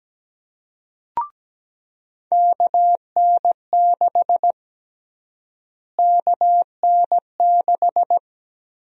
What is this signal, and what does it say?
Morse code, heard as a keyed tone, about 17 wpm, 700 Hz. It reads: KN6 KN6
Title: USA Call Sign Prefixes - Repeated 2x - 17wpm